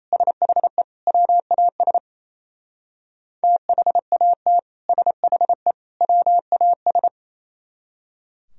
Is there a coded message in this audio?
Yes: Morse code from 0.1 to 7.1 s, heard as a keyed tone, about 36 wpm, 700 Hz. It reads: SHE WAS THAT SHE WAS